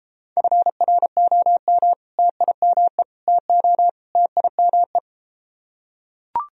Morse code, heard as a keyed tone, about 33 wpm, 700 Hz. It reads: FROM TIME TO TIME